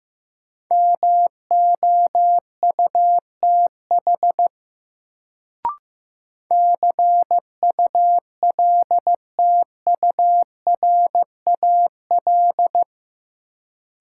Morse code, heard as a keyed tone, about 16 wpm, 700 Hz. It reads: MOUTH CULTURAL
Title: Sentences from Top 1500 Words - New Word Drill Alt - 15wpm